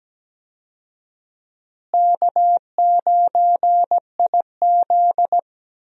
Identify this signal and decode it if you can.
Morse code, heard as a keyed tone, about 17 wpm, 700 Hz. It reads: K9IZ
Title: USA Call Signs - 4 Characters - Repeated 2x - 17wpm